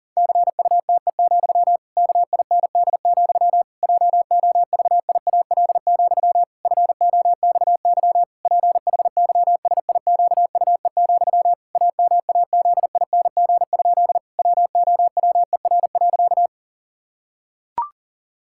Morse code, heard as a keyed tone, about 37 wpm, 700 Hz. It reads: CUTE, KIND, JOVIAL, FOXY PHYSIQUE, AMAZING? WOWER.
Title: Pangram Challenge v2 - 40wpm